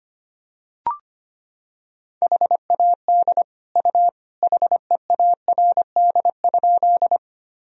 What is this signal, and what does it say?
Morse code, heard as a keyed tone, about 25 wpm, 700 Hz. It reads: HAD U HEARD?
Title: Sentences from Top 400 Words and Abbreviations 25wpm